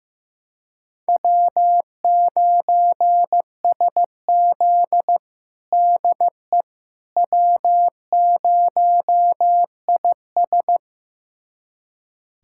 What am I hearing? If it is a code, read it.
Morse code, heard as a keyed tone, about 15 wpm, 700 Hz. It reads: W9SZ DE W0IS
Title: USA Callsign DE Callsign - 4 Characters - 15wpm